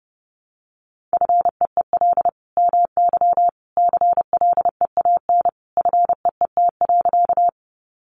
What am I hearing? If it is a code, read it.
Morse code, heard as a keyed tone, about 30 wpm, 700 Hz. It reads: FEEL MY CLEAN FEET.
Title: Sentences from Top 1100 Words 30wpm